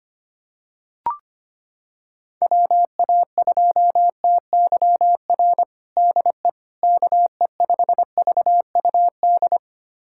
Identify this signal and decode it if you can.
Morse code, heard as a keyed tone, about 25 wpm, 700 Hz. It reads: WA2TYR DE KE5VUD
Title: USA Callsign DE Callsign - 6 Characters - 25wpm